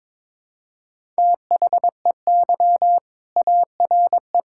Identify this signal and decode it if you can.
Morse code, heard as a keyed tone, about 22 wpm, 700 Hz. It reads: THEY ARE
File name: practice / 22wpm / Sentences with 4-Word Phrases - 22wpm